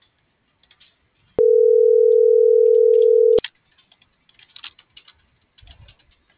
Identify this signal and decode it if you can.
Ringback tone (North America), 440 + 480 Hz.